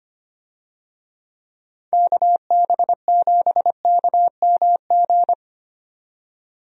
Morse code, heard as a keyed tone, about 25 wpm, 700 Hz. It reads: KB7KMG
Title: USA Call Signs - 6 Characters - 25wpm